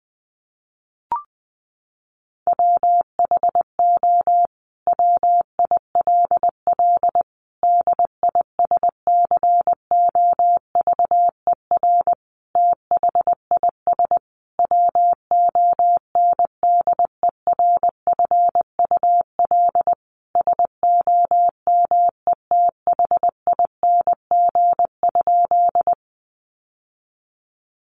Morse code, heard as a keyed tone, about 19 wpm, 700 Hz. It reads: WHO WILL DISCOVER THIS WONDERFUL SOMETHING?
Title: Sentences from Top 1100 Words 20wpm